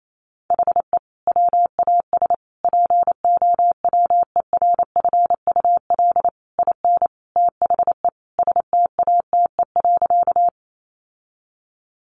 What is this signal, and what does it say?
Morse code, heard as a keyed tone, about 28 wpm, 700 Hz. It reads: HE WAS POWERFUL IN THE STATE.
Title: Sentences from Top 1300 Words 28wpm